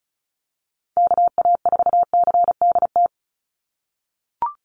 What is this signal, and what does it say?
Morse code, heard as a keyed tone, about 34 wpm, 700 Hz. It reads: KA4CDT
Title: USA Call Signs 35wpm